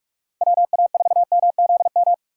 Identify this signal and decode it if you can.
Morse code, heard as a keyed tone, about 44 wpm, 700 Hz. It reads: WA4MZM
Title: USA Call Signs - 6 Characters - 2x Character Spacing - 45wpm